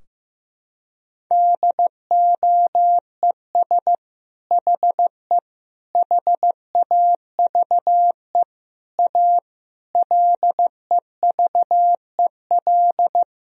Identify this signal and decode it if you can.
Morse code, heard as a keyed tone, about 15 wpm, 700 Hz. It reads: DOES HE HAVE A LEVEL
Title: Sentences from Top 1000 Words - Review Part 4 - 15wpm